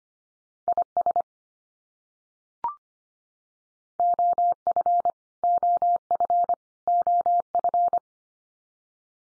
Morse code, heard as a keyed tone, about 25 wpm, 700 Hz. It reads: IS OF OF OF
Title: Top 100 Words 25wpm x3